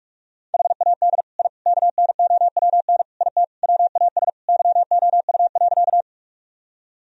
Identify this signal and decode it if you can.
Morse code, heard as a keyed tone, about 46 wpm, 700 Hz. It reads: HAD I KNOWN IT WAS YOU.
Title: Sentences from Top 1000 Words - Review Part 3 - 45wpm